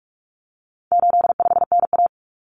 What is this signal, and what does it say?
Morse code, heard as a keyed tone, about 44 wpm, 700 Hz. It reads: 85NA
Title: Common Cut Numbers - 4-Digit 45wpm